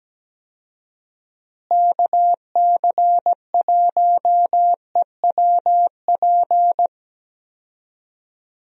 Morse code, heard as a keyed tone, about 17 wpm, 700 Hz. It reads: KC1EWP
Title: USA Call Signs - 6 Characters - Repeated 2x - 17wpm